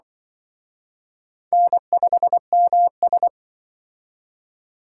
Morse code, heard as a keyed tone, about 25 wpm, 700 Hz. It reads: N5MS
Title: Speed Racing - USA 1x2 Call Signs 20wpm